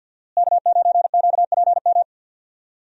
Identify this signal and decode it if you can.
Morse code, heard as a keyed tone, about 50 wpm, 700 Hz. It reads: K9QPM
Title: USA Call Signs - 5 Characters - Repeated 2x - 50wpm